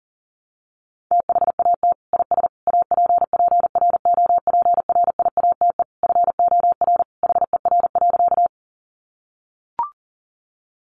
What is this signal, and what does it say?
Morse code, heard as a keyed tone, about 40 wpm, 700 Hz. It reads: THAT IS APPROPRIATE FOR HER.
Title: Sentences from Top 900 Words 40wpm